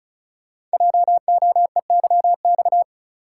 Morse code, heard as a keyed tone, about 36 wpm, 700 Hz. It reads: JOEYX
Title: Code Groups - 5 Letter:Numbers - 35wpm